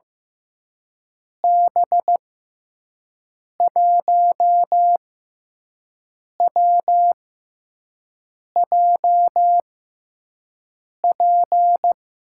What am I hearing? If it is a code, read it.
Morse code, heard as a keyed tone, about 15 wpm, spaced as for 5 wpm, 700 Hz. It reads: B1WJP
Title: USA Call Signs - 6 Characters - 6x Character Spacing - 15wpm